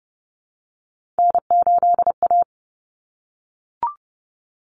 Morse code, heard as a keyed tone, about 30 wpm, 700 Hz. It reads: N8A
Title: Common Cut Numbers - 3-Digit 30wpm